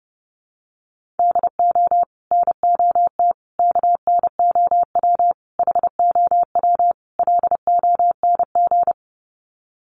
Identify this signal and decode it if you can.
Morse code, heard as a keyed tone, about 30 wpm, 700 Hz. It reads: DO NOT KNOW HOW LONG
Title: Sets of 5 Words Encore2 30wpm